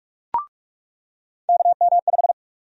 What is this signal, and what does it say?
Morse code, heard as a keyed tone, about 46 wpm, 700 Hz. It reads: KM5
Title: USA Call Sign Prefixes - 45wpm